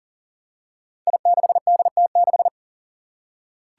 Morse code, heard as a keyed tone, about 37 wpm, 700 Hz. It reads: I6DT6
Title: Five Challenging Letter-Number Combinations 40wpm